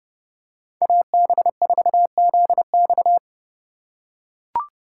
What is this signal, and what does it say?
Morse code, heard as a keyed tone, about 32 wpm, 700 Hz. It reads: AB4ZX
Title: USA Call Signs - 5 Characters - 2x Character Spacing - 30wpm